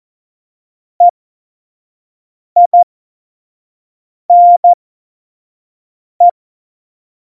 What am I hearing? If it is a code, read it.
Morse code, heard as a keyed tone, about 14 wpm, spaced as for 5 wpm, 700 Hz. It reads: EINE
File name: Morsekurs / Witze / Witz 19. Juni 2018, 25 F70